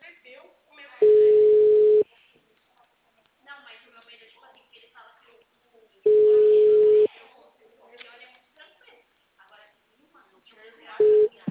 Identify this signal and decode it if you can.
Ringback tone (Europe, 425 Hz), with speech.
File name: out- -2021- - - .56